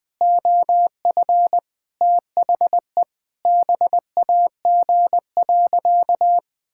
Morse code, heard as a keyed tone, about 21 wpm, 700 Hz. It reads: OF THE BAG.
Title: Sentences with 7-Word Phrases - Spoken Beginning - 20wpm